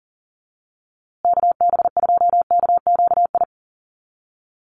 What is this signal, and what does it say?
Morse code, heard as a keyed tone, about 40 wpm, 700 Hz. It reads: KB2KQI
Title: USA Call Signs Repeated 3x 40wpm